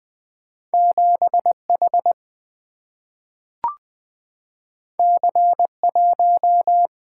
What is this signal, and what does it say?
Morse code, heard as a keyed tone, about 20 wpm, 700 Hz. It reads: 7H C1
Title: Single-Letter Single-Number 20wpm